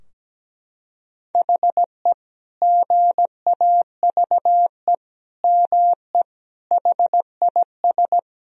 Morse code, heard as a keyed tone, about 17 wpm, 700 Hz. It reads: HE GAVE ME HIS
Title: Sentences from Top 1000 Words - Review Part 4 - 17wpm